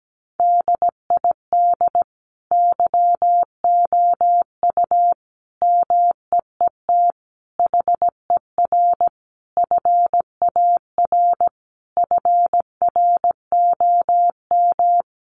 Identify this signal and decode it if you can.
Morse code, heard as a keyed tone, about 17 wpm, 700 Hz. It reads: DID YOU MEET HER FAR FROM
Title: Sentences from Top 1100 Words 17wpm